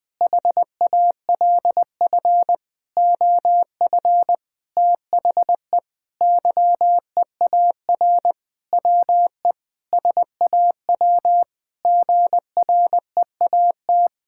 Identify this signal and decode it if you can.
Morse code, heard as a keyed tone, about 20 wpm, 700 Hz. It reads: HALF OF THE YEAR WE SAW GREAT
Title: Sentences with 7-Word Phrases - Spoken Beginning - 20wpm